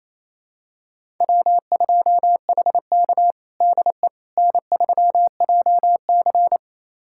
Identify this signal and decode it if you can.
Morse code, heard as a keyed tone, about 28 wpm, 700 Hz. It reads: W2HK DE N3JC
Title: USA Callsign DE Callsign - 4 Characters - 28wpm